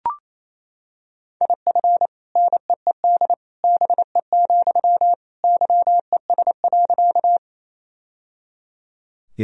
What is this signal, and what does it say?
Morse code, heard as a keyed tone, about 28 wpm, 700 Hz. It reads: IF NEED BE, YES.
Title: Sentences from Top 1000 Words - Review Part 1 - 28wpm